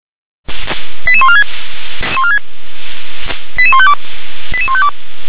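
Five-tone selective call (ZVEI), heard as quick runs of five tones, tones of about 70 ms. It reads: 88146 88152 88152
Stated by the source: ZVEI five-tone selective call 88853, received via SDR.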